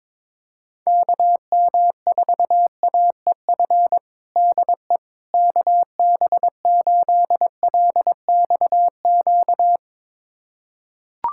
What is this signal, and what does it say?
Morse code, heard as a keyed tone, about 22 wpm, 700 Hz. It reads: KM4AEF DE KB8LXQ
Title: USA Callsign DE Callsign - 6 Characters - 22wpm